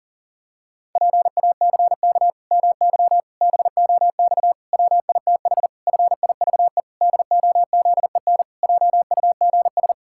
Morse code, heard as a keyed tone, about 37 wpm, 700 Hz. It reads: PACK MY BOX WITH FIVE DOZEN JUGS
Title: Pangram Challenge v2 - 40wpm